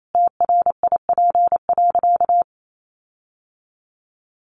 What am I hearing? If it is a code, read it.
Morse code, heard as a keyed tone, about 28 wpm, 700 Hz. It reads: TRIP.
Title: Sentences from Top 1300 Words 28wpm